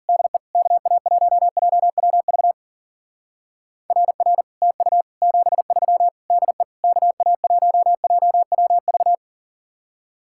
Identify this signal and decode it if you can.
Morse code, heard as a keyed tone, about 49 wpm, 700 Hz. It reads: DE KA1JWV RR TU 73 DE KA1JWV
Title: Speed-Racing Common QSO Phrases - 40wpm